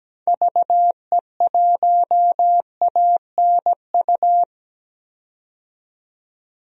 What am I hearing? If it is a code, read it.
Morse code, heard as a keyed tone, about 17 wpm, 700 Hz. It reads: VE1ANU
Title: DXCC-100 Call Signs - 6 Characters - 17wpm